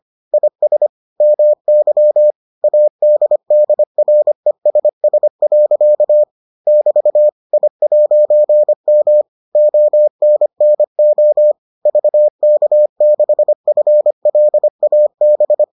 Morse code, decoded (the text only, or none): IS MY ADDRESS. <BT> I'M ONNO VK6FLAB